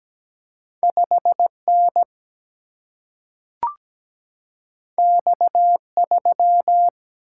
Morse code, heard as a keyed tone, about 17 wpm, 700 Hz. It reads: 5N X3
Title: Single-Letter Single-Number 17wpm